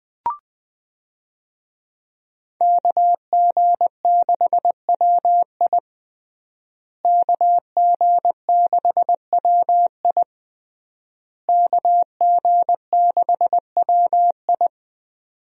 Morse code, heard as a keyed tone, about 21 wpm, 700 Hz. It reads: KG6WI KG6WI KG6WI
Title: USA Call Signs - 5 Characters - Repeated 3x - 20wpm